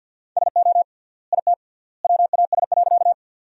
Morse code, heard as a keyed tone, about 50 wpm, 700 Hz. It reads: SO IT WAS.
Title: 3x Word-Spacing - Sentences from Top 100 Words - 50wpm